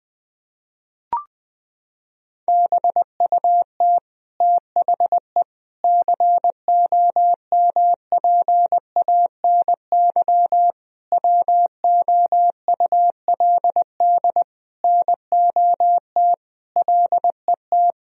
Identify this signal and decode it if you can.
Morse code, heard as a keyed tone, about 20 wpm, 700 Hz. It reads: BUT THE COMPANY WOULD NOT LET
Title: Sentences from Top 1000 Words - Review Part 3 - 20wpm